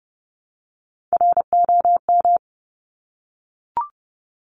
Morse code, heard as a keyed tone, about 30 wpm, 700 Hz. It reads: ROM
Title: Top Three Letter Combinations in Words 30wpm